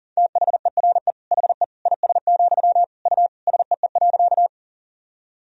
Morse code, heard as a keyed tone, about 40 wpm, 700 Hz. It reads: THERE HE IS, U SEE.
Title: Sentences from Top 100 Words and Abbreviations 40wpm